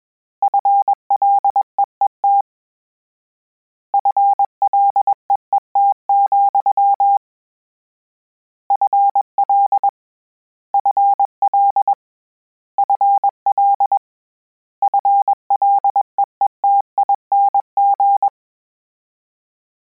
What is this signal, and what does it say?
Morse code, heard as a keyed tone, about 21 wpm, 800 Hz. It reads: FLEET FLEET, FL FL FL FLEETING
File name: Combinations / Combinations-mp3s / Combinations@20